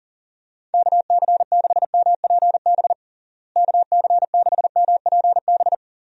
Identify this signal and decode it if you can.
Morse code, heard as a keyed tone, about 40 wpm, 700 Hz. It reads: KC6MPB KC6MPB
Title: USA Call Signs - 6 Characters - Repeated 3x - 40wpm